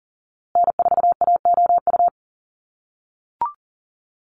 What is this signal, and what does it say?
Morse code, heard as a keyed tone, about 40 wpm, 700 Hz. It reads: N4AOU
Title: USA Call Signs Repeated 2x 40wpm